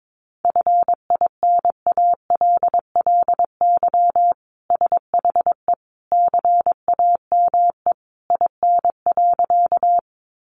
Morse code, heard as a keyed tone, about 22 wpm, 700 Hz. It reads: FINALLY SHE CAME IN.
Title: Sentences from Top 1000 Words 22wpm